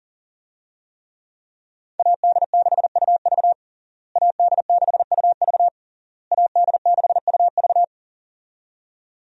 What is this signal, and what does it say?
Morse code, heard as a keyed tone, about 37 wpm, 700 Hz. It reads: AD6UV AD6UV AD6UV
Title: USA Call Signs - 5 Characters - Repeated 3x - 40wpm